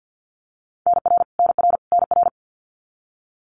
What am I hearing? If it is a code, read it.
Morse code, heard as a keyed tone, about 50 wpm, 700 Hz. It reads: NR NR NR
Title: Abbreviations Repeated 3x - 50wpm